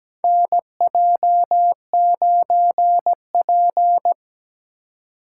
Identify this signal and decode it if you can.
Morse code, heard as a keyed tone, about 17 wpm, 700 Hz. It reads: NJ9P
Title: USA Call Signs - 4 Characters - 3x Character Spacing - 17wpm